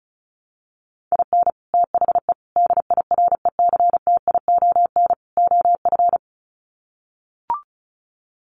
Morse code, heard as a keyed tone, about 34 wpm, 700 Hz. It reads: IN THE DIRECTION OF
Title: Sets of 4 Words 35wpm